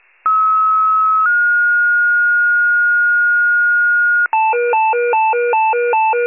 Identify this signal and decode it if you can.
Two-tone page, heard as a one-second tone, then a three-second tone, about 1290 then 1430 Hz. It is heard from 0.2 to 4.3 s.